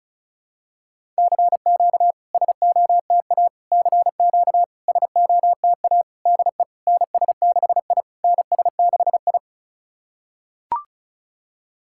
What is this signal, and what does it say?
Morse code, heard as a keyed tone, about 36 wpm, 700 Hz. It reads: CQ SOTA CQ SOTA DE NS6I NS6I
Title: SOTA QSO Practice 35wpm